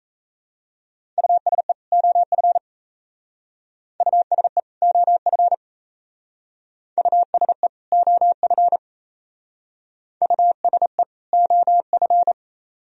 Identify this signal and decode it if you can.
Morse code, heard as a keyed tone, about 36 wpm, 700 Hz. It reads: USE OF USE OF USE OF USE OF